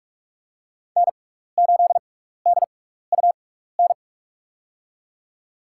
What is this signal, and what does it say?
Morse code, heard as a keyed tone, about 44 wpm, spaced as for 15 wpm, 700 Hz. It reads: N8DUN
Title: USA Call Signs - 5 Characters - 6x Character Spacing - 45wpm